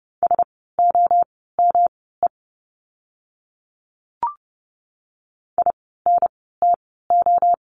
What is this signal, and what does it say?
Morse code, heard as a keyed tone, about 30 wpm, spaced as for 17 wpm, 700 Hz. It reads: SOME INTO
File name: course / 30:10wpm / 115 Lesson 19f - Letter w - Words 30wpm